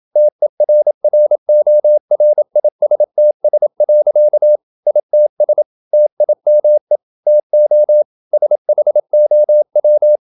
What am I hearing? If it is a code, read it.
Morse code, heard as a keyed tone, about 27 wpm, 600 Hz. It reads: TERRORISTS. ITS TIME TO SHOW
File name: Never 0027wpm